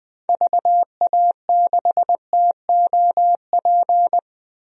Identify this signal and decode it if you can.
Morse code, heard as a keyed tone, about 20 wpm, 700 Hz. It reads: VA6TOP
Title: DXCC-100 Call Signs - 6 Characters - 20wpm